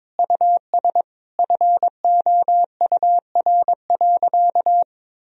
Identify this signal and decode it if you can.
Morse code, heard as a keyed tone, about 22 wpm, 700 Hz. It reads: US FOUR.
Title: Sentences from Top 1000 Words - Review Part 1 - 22wpm